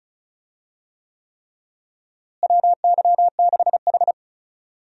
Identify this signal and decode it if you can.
Morse code, heard as a keyed tone, about 36 wpm, 700 Hz. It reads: WY6H
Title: USA Call Signs - 4 Characters - Repeated 3x - 35wpm